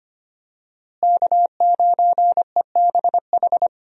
Morse code, heard as a keyed tone, about 25 wpm, 700 Hz. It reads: K9EBH
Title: USA Call Signs - 5 Characters - Repeated 2x - 25wpm